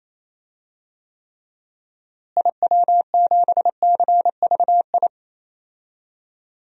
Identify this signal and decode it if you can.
Morse code, heard as a keyed tone, about 28 wpm, 700 Hz. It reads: IW7CVI
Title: DXCC-100 Call Signs - 6 Characters - Repeated 3x - 28wpm